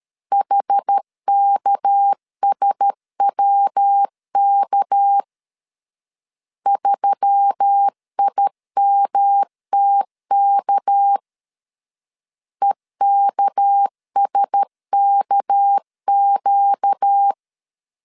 Morse code, heard as a keyed tone, about 13 wpm, 800 Hz. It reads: HKSWK 3IMTK EKSKQ
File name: LEZIONI / 6 PKX0 / 6 PK 12wpm 2006